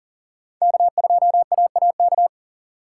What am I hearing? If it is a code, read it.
Morse code, heard as a keyed tone, about 40 wpm, 700 Hz. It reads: K2AAK
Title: USA Call Signs - 5 Characters - 2x Character Spacing - 40wpm